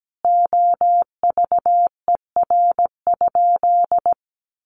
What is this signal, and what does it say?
Morse code, heard as a keyed tone, about 17 wpm, 700 Hz. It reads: OVER?
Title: Sentences from Top 1500 Words 17wpm